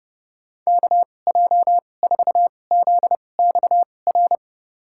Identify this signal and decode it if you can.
Morse code, heard as a keyed tone, about 30 wpm, spaced as for 22 wpm, 700 Hz. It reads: KJ4ZXR
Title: USA Call Signs - 6 Characters - 2x Character Spacing - 30wpm